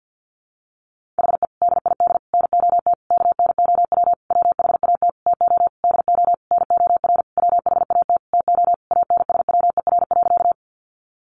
Morse code, heard as a keyed tone, about 50 wpm, 700 Hz. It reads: HE DID NOT KNOW WHAT TO DO NOR WHAT TO ANSWER.
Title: Sentences from Top 1500 Words 50wpm